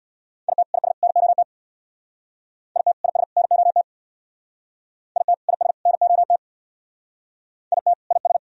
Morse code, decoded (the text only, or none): IT IS NOT IT IS NOT IT IS NOT IT IS